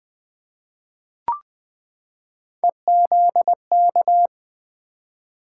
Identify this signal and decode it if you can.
Morse code, heard as a keyed tone, about 20 wpm, 700 Hz. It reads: EZK
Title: Three Letter-Number Combinations - 20wpm